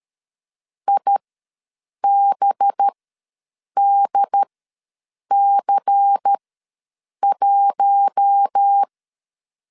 Morse code, heard as a keyed tone, about 13 wpm, spaced as for 7 wpm, 800 Hz. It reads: IBDC1
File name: LEZIONI / 8 YJ45 / 8 Y ws 12wpm 3409